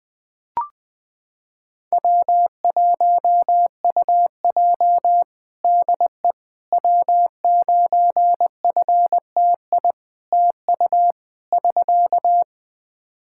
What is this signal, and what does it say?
Morse code, heard as a keyed tone, about 20 wpm, 700 Hz. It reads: W1UJ DE W9FTI TU <SK>
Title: Practice QSO - Exchange 4 - Ending - 20wpm